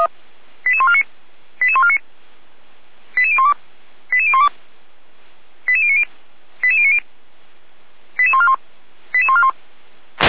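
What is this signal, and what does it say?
Five-tone selective call (ZVEI), heard as quick runs of five tones, tones of about 70 ms. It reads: 88149 88149 88012 88012 88090 88090 88141 88141
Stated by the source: ZVEI five-tone call received by SDR at Feuerwehr Orlamünde fire brigade.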